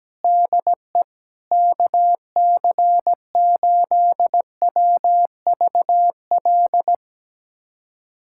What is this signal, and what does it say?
Morse code, heard as a keyed tone, about 17 wpm, 700 Hz. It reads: DE KC8WVL
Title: USA Callsign DE Callsign 17wpm